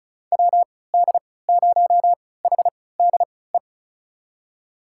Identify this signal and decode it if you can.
Morse code, heard as a keyed tone, about 34 wpm, spaced as for 20 wpm, 700 Hz. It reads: WD0HDE
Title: USA Call Signs - 3x Character Spacing - 35wpm